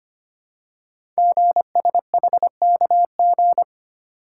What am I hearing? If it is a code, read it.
Morse code, heard as a keyed tone, about 25 wpm, 700 Hz. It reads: GSHKG
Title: Code Groups - 5 Letters - 25wpm